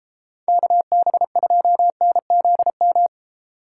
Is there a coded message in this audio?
Yes: Morse code from 0.5 to 3.1 s, heard as a keyed tone, about 30 wpm, 700 Hz. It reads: KB2NZM